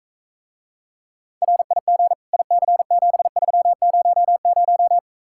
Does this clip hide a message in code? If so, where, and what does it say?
Yes: Morse code from 1.4 to 5.0 s, heard as a keyed tone, about 45 wpm, 700 Hz. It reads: RIG IC7300